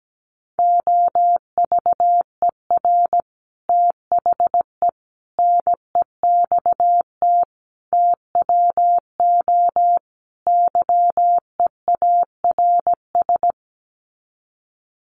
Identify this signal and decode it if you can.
Morse code, heard as a keyed tone, about 17 wpm, 700 Hz. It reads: OVER THE NEXT TWO YEARS
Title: Sets of 5 Words Encore2 17wpm